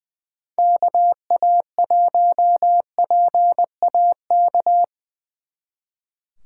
Morse code, heard as a keyed tone, about 23 wpm, 700 Hz. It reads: KA1PAK
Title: Speed Racing - USA 2x3 Call Signs 20wpm